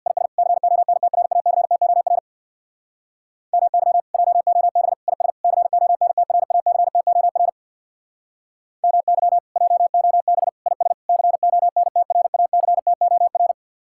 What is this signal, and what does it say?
Morse code, heard as a keyed tone, about 56 wpm, 700 Hz. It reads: IS CONTRACTOR MY JOB IS CONTRACTOR MY JOB IS CONTRACTOR
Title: Speed-Racing Common QSO Phrases Encore - 50wpm